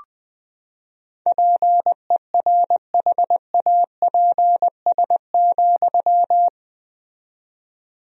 Morse code, heard as a keyed tone, about 20 wpm, 700 Hz. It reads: PERHAPS,
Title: 4x Word-Spacing - Sentences from Top 500 Words - 20wpm